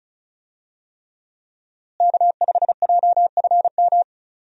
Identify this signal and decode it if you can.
Morse code, heard as a keyed tone, about 34 wpm, 700 Hz. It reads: K5JFM
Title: USA Call Signs - 5 Characters - Repeated 3x - 35wpm